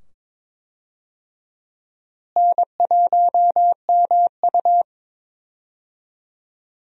Morse code, heard as a keyed tone, about 22 wpm, 700 Hz. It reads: N1MU
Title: USA Call Signs - 4 Characters - Repeated 2x - 22wpm